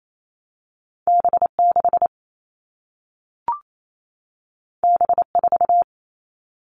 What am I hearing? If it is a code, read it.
Morse code, heard as a keyed tone, about 28 wpm, 700 Hz. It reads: B6 B4
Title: All Cut Numbers - 2-Digit 28wpm